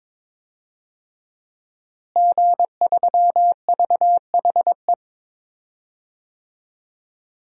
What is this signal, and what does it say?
Morse code, heard as a keyed tone, about 22 wpm, 700 Hz. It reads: G3VHE
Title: DXCC-100 Call Signs - 5 Characters - Repeated 2x - 22wpm